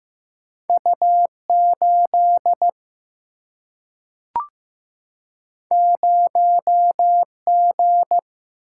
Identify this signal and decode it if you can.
Morse code, heard as a keyed tone, about 15 wpm, 700 Hz. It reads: U8 0G
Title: Single-Letter Single-Number 15wpm